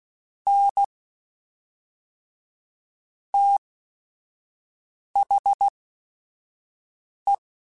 Morse code, heard as a keyed tone, about 16 wpm, spaced as for 5 wpm, 800 Hz. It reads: N THE
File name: Technical Subjects / Technical Subjects-mp3s / XL @5